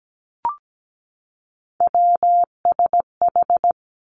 Morse code, heard as a keyed tone, about 17 wpm, 700 Hz. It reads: WSH
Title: Three Challenging Letter-Number Combinations 17wpm